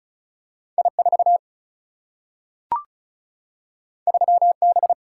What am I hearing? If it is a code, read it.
Morse code, heard as a keyed tone, about 36 wpm, 700 Hz. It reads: I 4 3 B
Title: Single-Letter Single-Number 35wpm